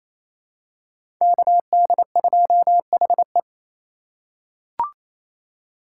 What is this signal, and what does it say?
Morse code, heard as a keyed tone, about 27 wpm, 700 Hz. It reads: KD2HE
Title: K1USN SST QSO Practice - 28wpm